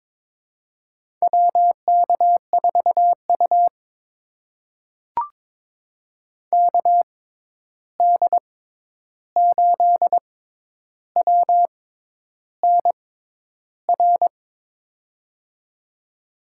Morse code, heard as a keyed tone, about 22 wpm, spaced as for 10 wpm, 700 Hz. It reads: WK4U KD8WNR
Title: USA Call Signs - 6x Character Spacing - 22wpm